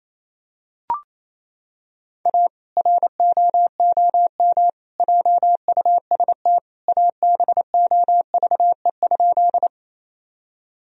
Morse code, heard as a keyed tone, about 28 wpm, 700 Hz. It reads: A ROOM JUST ABOVE?
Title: Sentences from Top 1000 Words - Review Part 2 - 28wpm